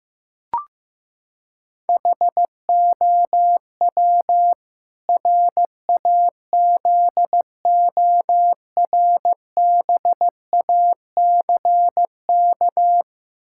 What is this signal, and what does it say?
Morse code, heard as a keyed tone, about 15 wpm, 700 Hz. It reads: HOW RAZORBACK
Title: Pangram Challenge v2 - 15wpm